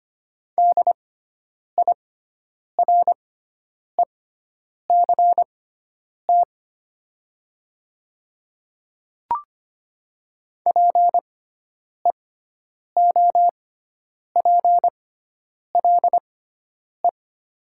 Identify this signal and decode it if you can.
Morse code, heard as a keyed tone, about 25 wpm, spaced as for 9 wpm, 700 Hz. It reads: DIRECT PEOPLE
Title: 6x Character Spacing - Top 500 Words - 25wpm